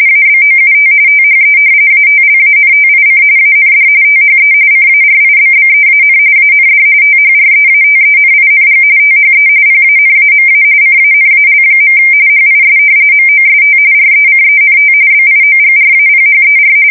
RTTY, about 45.45 baud. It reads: ABOUT THE ATLANTIC AND THE CARIBBEAN AND THREATENING THE PANAMA CANAL. HIS MOST RECENT BIOGRAPHER
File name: fsk170 / usnavy4